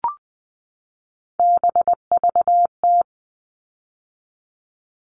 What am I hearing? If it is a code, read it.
Morse code, heard as a keyed tone, about 20 wpm, 700 Hz. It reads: BVT
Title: All Cut Numbers - 3-Digit 20wpm